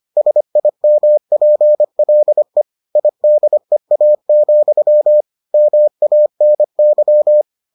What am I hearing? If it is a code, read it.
Morse code, heard as a keyed tone, about 25 wpm, 600 Hz. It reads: SIMPLE IDEA, MANY